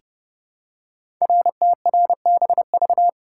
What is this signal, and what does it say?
Morse code, heard as a keyed tone, about 28 wpm, 700 Hz. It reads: RTRBV
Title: Code Groups - 5 Letters - 30wpm